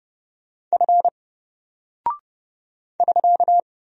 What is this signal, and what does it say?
Morse code, heard as a keyed tone, about 30 wpm, 700 Hz. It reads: F <SK>
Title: Single Character (Letter:Number:Punctuation:Prosign) 30wpm